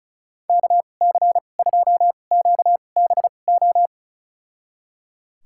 Morse code, heard as a keyed tone, about 36 wpm, spaced as for 25 wpm, 700 Hz. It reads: KC2QBO
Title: USA Call Signs - 2x Character Spacing - 35wpm